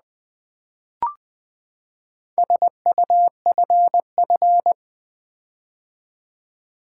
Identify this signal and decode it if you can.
Morse code, heard as a keyed tone, about 20 wpm, 700 Hz. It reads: SUFF
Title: Build A Word - 4 Letters At A Time - 12-Letter Words - 20wpm